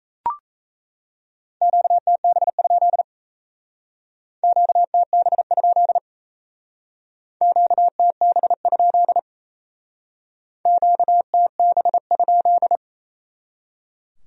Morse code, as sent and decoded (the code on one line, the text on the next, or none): --.- - -... ..--.. / --.- - -... ..--.. / --.- - -... ..--.. / --.- - -... ..--..
QTB? QTB? QTB? QTB?